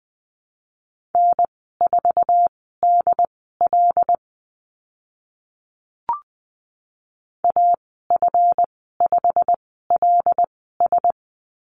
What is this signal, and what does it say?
Morse code, heard as a keyed tone, about 20 wpm, spaced as for 14 wpm, 700 Hz. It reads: N4DL AF5LS
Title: 140 Lesson 23g - Letter f - Call Signs 20wpm